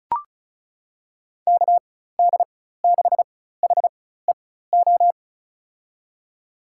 Morse code, heard as a keyed tone, about 34 wpm, spaced as for 16 wpm, 700 Hz. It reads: KD6HEO
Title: USA Call Signs - 4x Character Spacing - 35wpm